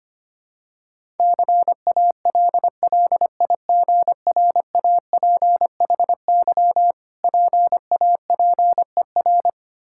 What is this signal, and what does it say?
Morse code, heard as a keyed tone, about 25 wpm, 690 Hz. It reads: CALLIGRAPHY PAPER
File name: practice / 25wpm / Difficult Words with Hints - 25wpm